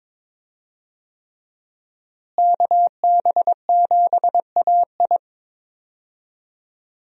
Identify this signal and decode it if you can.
Morse code, heard as a keyed tone, about 21 wpm, 700 Hz. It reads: KB7AI
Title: USA Call Signs - 5 Characters - Repeated 3x - 22wpm